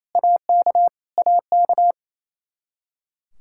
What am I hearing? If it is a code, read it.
Morse code, heard as a keyed tone, about 28 wpm, 700 Hz. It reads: AK AK
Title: Practice QSO - Exchange 1 - Beginning - 28wpm